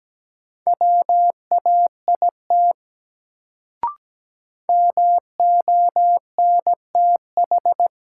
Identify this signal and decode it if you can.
Morse code, heard as a keyed tone, about 17 wpm, 700 Hz. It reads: WAIT MONTH
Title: Sets of 5 Words Encore - New Word Drill - 17wpm